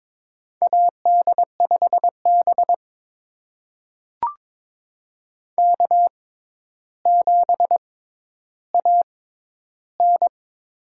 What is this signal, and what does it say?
Morse code, heard as a keyed tone, about 22 wpm, spaced as for 11 wpm, 700 Hz. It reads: AD5B K7AN
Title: USA Call Signs - 4 Characters - 6x Character Spacing - 22wpm